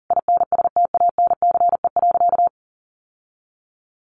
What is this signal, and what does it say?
Morse code, heard as a keyed tone, about 40 wpm, 700 Hz. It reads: INSTANCE.
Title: Sentences from Top 800 Words 40wpm